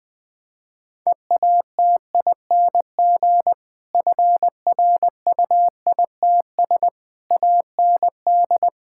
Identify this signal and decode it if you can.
Morse code, heard as a keyed tone, about 20 wpm, 700 Hz. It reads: EATING FRUITS AND
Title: Difficult Sentences with Hints - 20wpm